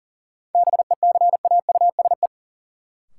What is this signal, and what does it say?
Morse code, heard as a keyed tone, about 40 wpm, 700 Hz. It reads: BECAUSE
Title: Sets of 2 Words - New Word Drill - 40wpm